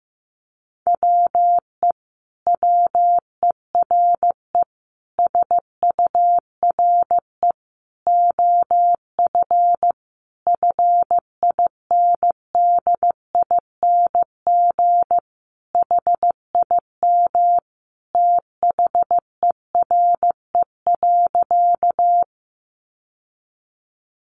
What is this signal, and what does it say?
Morse code, heard as a keyed tone, about 15 wpm, 700 Hz. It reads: WE WERE SURE OF FINDING HIM THERE.
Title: Sentences from Top 1100 Words 15wpm